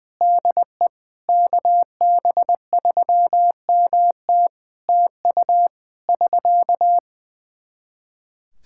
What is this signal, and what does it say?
Morse code, heard as a keyed tone, about 21 wpm, 700 Hz. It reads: DE KB3MT TU <SK>
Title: K1USN SST QSO Practice 20wpm